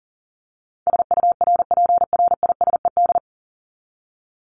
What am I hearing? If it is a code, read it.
Morse code, heard as a keyed tone, about 40 wpm, 700 Hz. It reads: SURPRISED
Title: Top 1501-1750 Words 40wpm